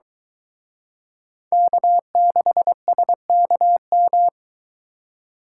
Morse code, heard as a keyed tone, about 23 wpm, 700 Hz. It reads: K6SKM